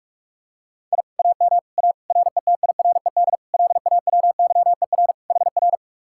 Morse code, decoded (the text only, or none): I AM A RETIRED LAWYER HR